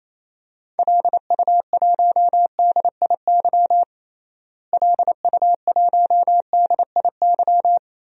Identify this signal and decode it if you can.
Morse code, heard as a keyed tone, about 28 wpm, 700 Hz. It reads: LU1DIY LU1DIY
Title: DXCC-100 Call Signs - 6 Characters - Repeated 3x - 28wpm